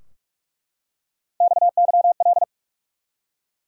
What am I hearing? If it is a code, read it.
Morse code, heard as a keyed tone, about 43 wpm, 700 Hz. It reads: XYL
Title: Top 180 QSO Elements 45wpm